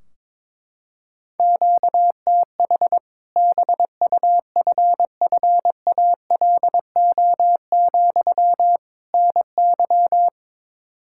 Morse code, heard as a keyed tone, about 21 wpm, 700 Hz. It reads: QTH BUFFALO, NY
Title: Practice QSO - Exchange 1 - Beginning - 22wpm